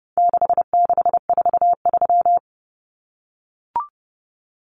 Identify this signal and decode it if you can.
Morse code, heard as a keyed tone, about 30 wpm, 700 Hz. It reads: 6643
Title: Common Cut Numbers - 4-Digit 30wpm